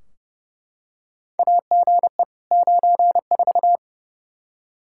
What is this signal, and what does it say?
Morse code, heard as a keyed tone, about 29 wpm, 700 Hz. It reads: AGE 94
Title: Practice QSO - Exchange 3 - More Info - 30wpm